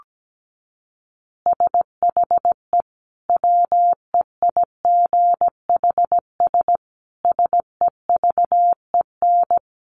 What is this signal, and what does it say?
Morse code, heard as a keyed tone, about 17 wpm, 700 Hz. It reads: SHE WEIGHS SEVEN
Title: Easy Phrases 17wpm